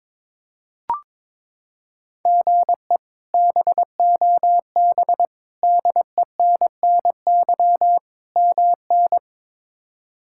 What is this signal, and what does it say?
Morse code, heard as a keyed tone, about 22 wpm, 700 Hz. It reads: GE BOB DENNY MN
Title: K1USN SST QSO Practice - 22wpm